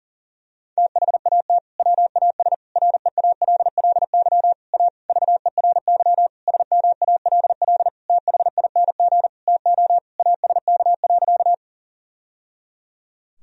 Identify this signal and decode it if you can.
Morse code, heard as a keyed tone, about 40 wpm, 700 Hz. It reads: THAT WAS REALLY A VERY SMALL THING TO ASK.
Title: Sentences from Top 1000 Words - Review Part 1 - 40wpm